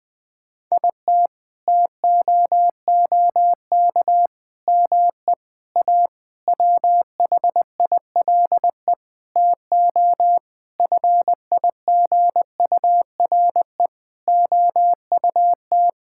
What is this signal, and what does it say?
Morse code, heard as a keyed tone, about 20 wpm, 700 Hz. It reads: IT TOOK ME A WHILE TO FIGURE OUT
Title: Sentences with 8-Word Phrases - 20wpm